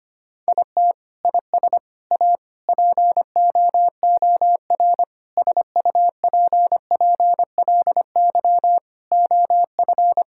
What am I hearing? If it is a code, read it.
Morse code, heard as a keyed tone, about 25 wpm, 700 Hz. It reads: IT IS A POOR SUPPLY OF
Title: Sentences from Top 1000 Words - Review Part 5 - 25wpm